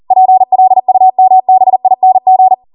Morse code, heard as a keyed tone, about 40 wpm, 750 Hz. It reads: PLUMBING
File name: Morse / 40WPM